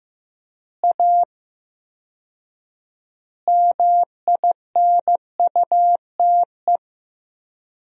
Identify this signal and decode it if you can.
Morse code, heard as a keyed tone, about 15 wpm, 700 Hz. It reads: A MINUTE
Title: 4x Word-Spacing - Sentences from Top 500 Words - 15wpm